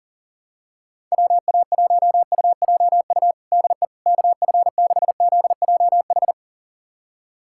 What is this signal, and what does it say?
Morse code, heard as a keyed tone, about 40 wpm, 700 Hz. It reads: WA1UJU DE KF6ZJH
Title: USA Callsign DE Callsign 40wpm